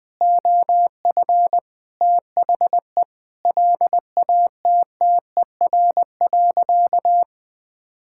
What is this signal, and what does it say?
Morse code, heard as a keyed tone, about 20 wpm, 700 Hz. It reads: OF THE LATTER.
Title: Sentences from Top 1600 Words 20wpm